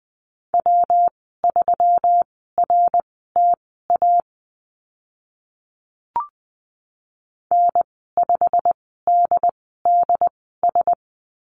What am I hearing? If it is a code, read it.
Morse code, heard as a keyed tone, about 20 wpm, spaced as for 14 wpm, 700 Hz. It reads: W3RTA N5DDS
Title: 122 Lesson 20g - Number 3 - Call Signs 20wpm